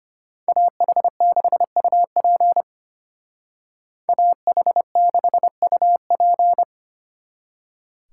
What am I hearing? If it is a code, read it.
Morse code, heard as a keyed tone, about 33 wpm, 700 Hz. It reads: AH6UP AH6UP